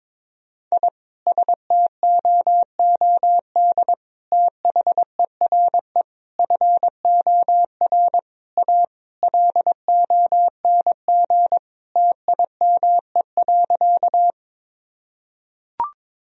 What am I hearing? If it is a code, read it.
Morse code, heard as a keyed tone, about 22 wpm, 700 Hz. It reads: I STOOD THERE FOR A LONG TIME.
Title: Sentences from Top 1000 Words - Review Part 2 - 22wpm